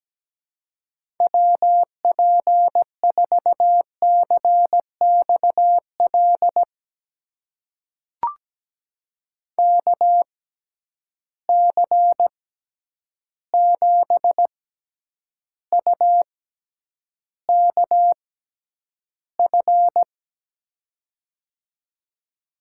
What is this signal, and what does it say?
Morse code, heard as a keyed tone, about 17 wpm, spaced as for 9 wpm, 700 Hz. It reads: WP4CXL KC7UKF
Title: USA Call Signs - 6 Characters - 6x Character Spacing - 17wpm